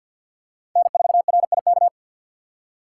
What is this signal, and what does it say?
Morse code, heard as a keyed tone, about 52 wpm, 700 Hz. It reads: N4RIK
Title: USA Call Signs - 5 Characters - 3x Character Spacing - 50wpm